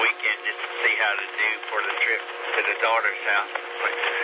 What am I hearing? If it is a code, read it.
Emergency Alert System (SAME) tones, heard as harsh buzzing bursts, with speech. It is heard from 0.8 to 3.4 s.